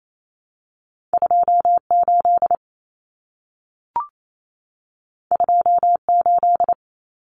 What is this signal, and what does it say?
Morse code, heard as a keyed tone, about 28 wpm, 700 Hz. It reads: 28 28
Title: Numbers - 2-Digit 28wpm